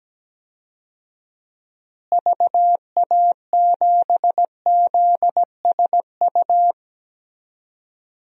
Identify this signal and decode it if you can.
Morse code, heard as a keyed tone, about 17 wpm, 700 Hz. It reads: VA7ZSU
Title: DXCC-100 Call Signs - 6 Characters - Repeated 2x - 17wpm